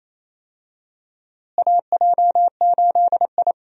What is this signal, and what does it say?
Morse code, heard as a keyed tone, about 28 wpm, 700 Hz. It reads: AJ8I
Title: USA Call Signs - 4 Characters - Repeated 2x - 28wpm